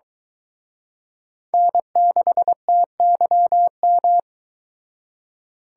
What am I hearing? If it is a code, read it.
Morse code, heard as a keyed tone, about 23 wpm, 700 Hz. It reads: N6TYM